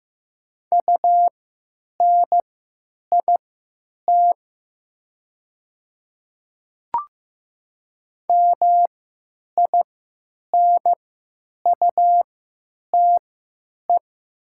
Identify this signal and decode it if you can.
Morse code, heard as a keyed tone, about 15 wpm, spaced as for 9 wpm, 700 Hz. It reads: UNIT MINUTE
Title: 3x Character Spacing - Top 500 Words - 15wpm